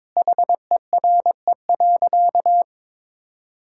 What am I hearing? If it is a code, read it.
Morse code, heard as a keyed tone, about 22 wpm, 700 Hz. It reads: HERE.
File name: practice / 22wpm / Sentences with 4-Word Phrases - 22wpm